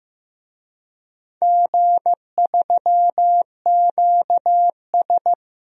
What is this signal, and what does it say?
Morse code, heard as a keyed tone, about 15 wpm, 700 Hz. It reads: G3QS
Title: DXCC-100 Call Signs - 4 Characters - Repeated 2x - 15wpm